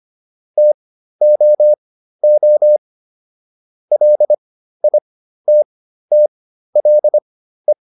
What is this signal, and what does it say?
Morse code, heard as a keyed tone, about 25 wpm, spaced as for 13 wpm, 600 Hz. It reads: TOO LITTLE